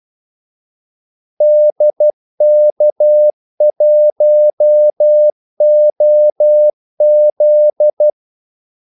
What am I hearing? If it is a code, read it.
Morse code, heard as a keyed tone, about 12 wpm, 600 Hz. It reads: DK1OZ
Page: 2 ekstra ordmellemrum mellem kaldesignalerne